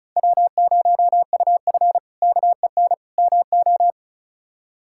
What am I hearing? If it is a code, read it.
Morse code, heard as a keyed tone, about 36 wpm, 700 Hz. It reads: W0UF KEN MO
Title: K1USN SST QSO Practice - 35wpm